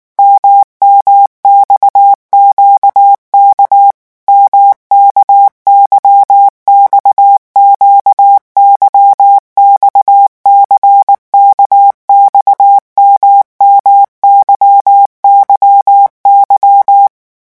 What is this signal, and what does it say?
Morse code, heard as a keyed tone, about 19 wpm, 800 Hz. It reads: MMXQK MKYXQ YXCKX MMYYY
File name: revision1-vitesse 18 mots